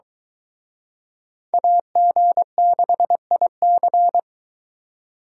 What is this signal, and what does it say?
Morse code, heard as a keyed tone, about 23 wpm, 700 Hz. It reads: AG6IC